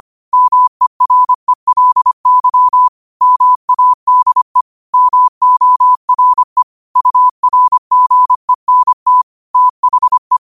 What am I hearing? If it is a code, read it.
Morse code, heard as a keyed tone, about 25 wpm, 1000 Hz. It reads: MERELY MADE MORE URGENT THE
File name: qotdfiles / 25wpm